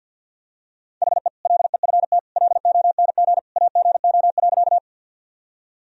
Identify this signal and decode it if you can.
Morse code, heard as a keyed tone, about 52 wpm, 700 Hz. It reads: HE LEFT LONG AGO.
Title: Sentences from Top 1000 Words - Review Part 3 - 50wpm